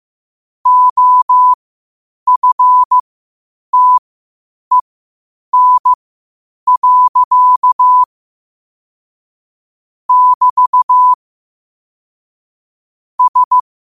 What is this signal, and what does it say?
Morse code, heard as a keyed tone, about 15 wpm, spaced as for 8 wpm, 1000 Hz. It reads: OFTEN. <BT> S
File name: qotdfiles / 8wpm